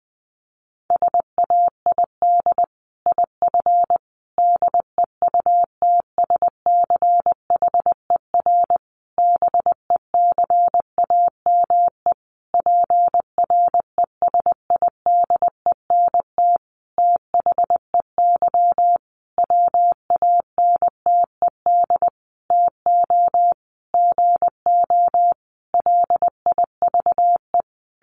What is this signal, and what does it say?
Morse code, heard as a keyed tone, about 20 wpm, 700 Hz. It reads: SAID IF DEUTSCHER BECAME PRESIDENT THEY WANTED TO GO LIVE
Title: A Sound Of Thunder 20wpm